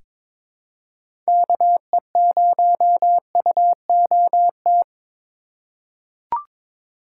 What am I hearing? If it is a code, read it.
Morse code, heard as a keyed tone, about 22 wpm, 700 Hz. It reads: KE0UOT
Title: USA Call Signs - 6 Characters - 4x Character Spacing - 22wpm